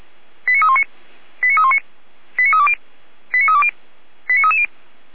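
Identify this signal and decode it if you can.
Five-tone selective call (ZVEI), heard as quick runs of five tones, tones of about 70 ms. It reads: 89319 89319 89320 89320 89330
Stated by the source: ZVEI five-tone selective call received on SDR.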